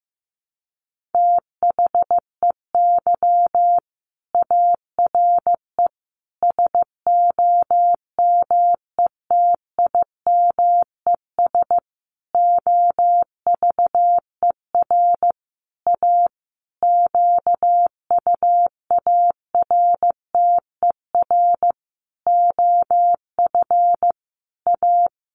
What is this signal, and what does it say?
Morse code, heard as a keyed tone, about 15 wpm, 700 Hz. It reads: THEY ARE SOMETIMES OVER A QUARTER OF A
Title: Sentences from Top 1300 Words 15wpm